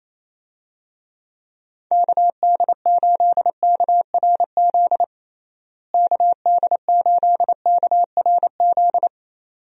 Morse code, heard as a keyed tone, about 28 wpm, 700 Hz. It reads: KD8KRZ KD8KRZ
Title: USA Call Signs - 6 Characters - Repeated 2x - 28wpm